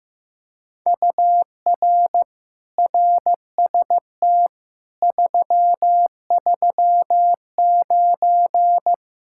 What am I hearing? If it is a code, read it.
Morse code, heard as a keyed tone, about 15 wpm, 700 Hz. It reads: UR RST 339
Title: Speed-Racing Common QSO Phrases Encore - 15wpm